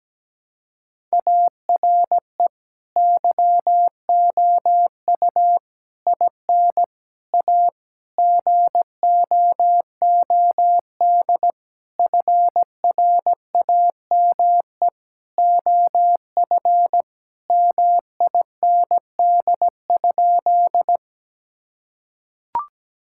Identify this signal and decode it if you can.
Morse code, heard as a keyed tone, about 17 wpm, 700 Hz. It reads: ARE YOU IN A GOOD FRAME OF MIND?
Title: Sentences from Top 1000 Words - Review Part 6 - 17wpm